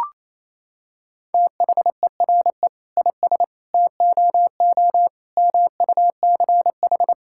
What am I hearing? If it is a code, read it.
Morse code, heard as a keyed tone, about 28 wpm, 700 Hz. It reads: THERE IS TOO MUCH
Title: Sentences from Top 1000 Words - Review Part 5 - 28wpm